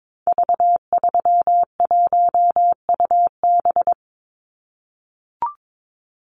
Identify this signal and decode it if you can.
Morse code, heard as a keyed tone, about 22 wpm, 700 Hz. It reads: V31UB
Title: Call Signs - 22wpm